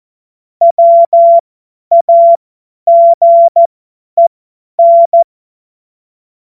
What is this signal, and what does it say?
Morse code, heard as a keyed tone, about 14 wpm, spaced as for 10 wpm, 680 Hz. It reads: WAGEN
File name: Morsekurs / Witze / Witz 08. August 2023 50 F70